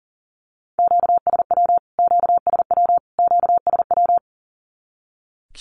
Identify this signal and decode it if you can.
Morse code, heard as a keyed tone, about 40 wpm, 700 Hz. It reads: QSW QSW QSW
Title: Q-Codes Repeated 3x 40wpm